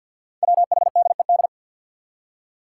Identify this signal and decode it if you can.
Morse code, heard as a keyed tone, about 47 wpm, 700 Hz. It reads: WHDED
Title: Five Challenging Letter-Number Combinations 50wpm